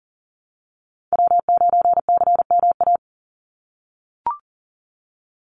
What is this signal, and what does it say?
Morse code, heard as a keyed tone, about 40 wpm, 700 Hz. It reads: W9CMA
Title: USA Call Signs 40wpm